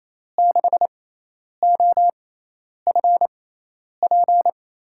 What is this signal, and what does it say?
Morse code, heard as a keyed tone, about 28 wpm, spaced as for 10 wpm, 700 Hz. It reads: 6OFP
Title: USA Call Signs - 5 Characters - 6x Character Spacing - 28wpm